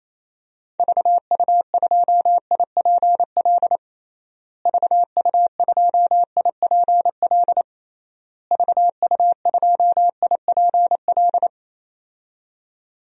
Morse code, heard as a keyed tone, about 28 wpm, 700 Hz. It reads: VU2IPL VU2IPL VU2IPL
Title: DXCC-100 Call Signs - 6 Characters - Repeated 3x - 28wpm